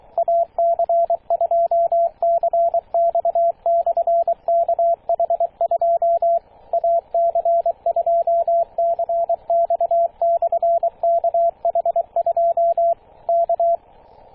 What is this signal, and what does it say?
Morse code, heard as a keyed tone, about 23 wpm, 680 Hz, with radio noise. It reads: AC2CX/KH2 AC2CX/KH2 K
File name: posts / 2016-june-26-0520z-ac2cx-kh2-21-027mhz-cw